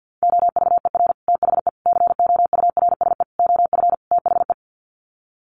Morse code, heard as a keyed tone, about 50 wpm, 700 Hz. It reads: OVER THE COURSE OF THE
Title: Sets of 5 Words 50wpm